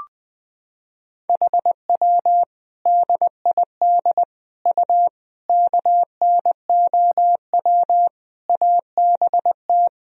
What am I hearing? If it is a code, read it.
Morse code, heard as a keyed tone, about 20 wpm, 700 Hz. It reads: HW DID U KNOW ABT
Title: Sentences from Top 100 Words and Abbreviations 20wpm